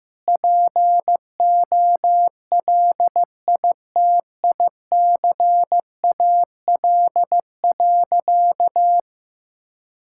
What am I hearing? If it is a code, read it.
Morse code, heard as a keyed tone, about 15 wpm, 700 Hz. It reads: POLITICAL.
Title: Sentences from Top 1000 Words - Review Part 5 - 15wpm